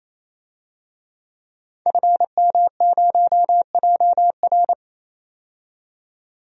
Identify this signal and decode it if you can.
Morse code, heard as a keyed tone, about 28 wpm, 700 Hz. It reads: FM0JR
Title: DXCC-100 Call Signs - 5 Characters - 28wpm